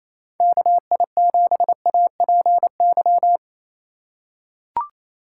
Morse code, heard as a keyed tone, about 28 wpm, 700 Hz. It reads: KI7APY
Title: USA Call Signs - 2x Character Spacing - 28wpm